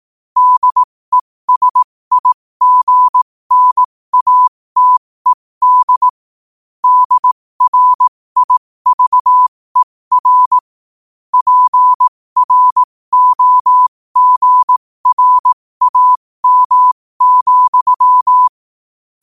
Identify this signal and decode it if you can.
Morse code, heard as a keyed tone, about 18 wpm, spaced as for 15 wpm, 1000 Hz. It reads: DESIGNATED DRIVER PROGRAM,
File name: 15wpm